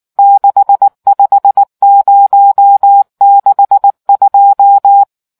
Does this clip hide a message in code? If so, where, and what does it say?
Yes: Morse code from 0.2 to 5.0 s, heard as a keyed tone, about 19 wpm, 800 Hz. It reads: 65062